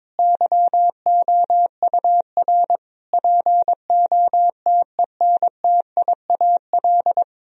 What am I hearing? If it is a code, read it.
Morse code, heard as a keyed tone, about 22 wpm, 700 Hz. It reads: YOUR POTENTIAL
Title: Sentences from Top 1000 Words - Review Part 4 - 22wpm